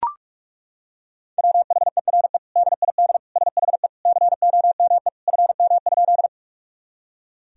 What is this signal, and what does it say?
Morse code, heard as a keyed tone, about 42 wpm, 700 Hz. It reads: WHERE DID SHE COME FM?
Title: Sentences from Top 200 Words and Abbreviations 45wpm